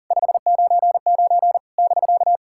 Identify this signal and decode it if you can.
Morse code, heard as a keyed tone, about 39 wpm, 700 Hz. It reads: 599 <BK>
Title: SOTA QSO Practice 40wpm